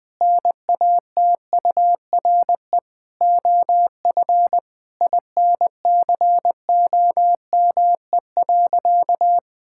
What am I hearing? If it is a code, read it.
Morse code, heard as a keyed tone, about 20 wpm, 700 Hz. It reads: NATURE OF INCOME.
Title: Sentences from Top 1000 Words - Review Part 5 - 20wpm